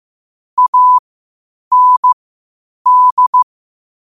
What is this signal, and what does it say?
Morse code, heard as a keyed tone, about 15 wpm, spaced as for 8 wpm, 1000 Hz. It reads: AND
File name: qotdfiles / 8wpm